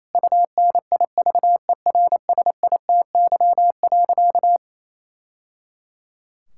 Morse code, heard as a keyed tone, about 28 wpm, 700 Hz. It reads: UNIVERSITY.
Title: Sentences from Top 1000 Words - Review Part 5 - 28wpm